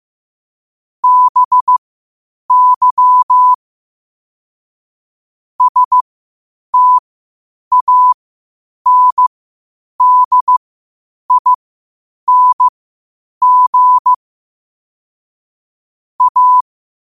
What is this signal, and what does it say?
Morse code, heard as a keyed tone, about 15 wpm, spaced as for 8 wpm, 1000 Hz. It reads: BY STANDING A
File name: qotdfiles / 8wpm